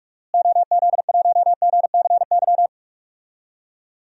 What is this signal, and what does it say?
Morse code, heard as a keyed tone, about 46 wpm, 700 Hz. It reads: OZ1GCY
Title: DXCC-100 Call Signs - 6 Characters - Repeated 3x - 45wpm